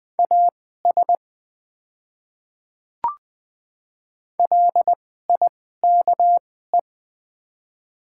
Morse code, heard as a keyed tone, about 20 wpm, spaced as for 15 wpm, 700 Hz. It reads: AS LIKE
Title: 2x Character Spacing - Top 100 Words - 20wpm